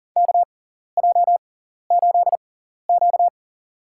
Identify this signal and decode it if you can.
Morse code, heard as a keyed tone, about 40 wpm, spaced as for 14 wpm, 700 Hz. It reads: KJ8Q
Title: USA Call Signs - 4 Characters - 6x Character Spacing - 40wpm